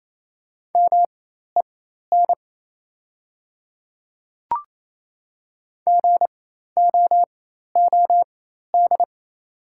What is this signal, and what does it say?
Morse code, heard as a keyed tone, about 28 wpm, spaced as for 13 wpm, 690 Hz. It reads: MEN GOOD
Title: 4x Character Spacing - Top 200 Words - 28wpm